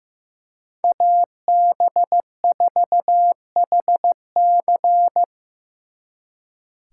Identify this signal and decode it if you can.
Morse code, heard as a keyed tone, about 18 wpm, 700 Hz. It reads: AB4HC
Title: Speed Racing - USA 2x2 Call Signs 15wpm